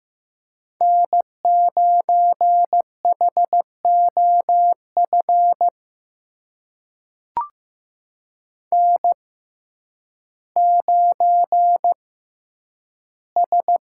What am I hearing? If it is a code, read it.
Morse code, heard as a keyed tone, about 15 wpm, spaced as for 9 wpm, 700 Hz. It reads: N9HOF N9S
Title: USA Call Signs - 5 Characters - 6x Character Spacing - 15wpm